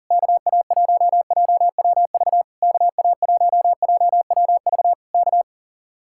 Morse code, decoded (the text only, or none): KA1JWV KA1JWV K